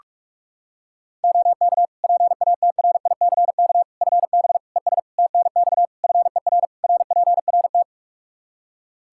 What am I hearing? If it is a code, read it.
Morse code, heard as a keyed tone, about 46 wpm, 700 Hz. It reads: OK PATRICK FB ES TNX FER RPRT
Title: Practice QSO - Exchange 2 - Info - 45wpm